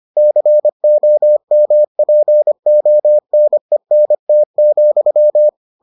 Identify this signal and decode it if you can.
Morse code, heard as a keyed tone, about 25 wpm, 600 Hz. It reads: COMPONENT,